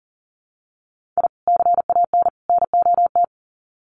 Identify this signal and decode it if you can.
Morse code, heard as a keyed tone, about 40 wpm, 700 Hz. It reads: I CAN NOT
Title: Sets of 3 Words 40wpm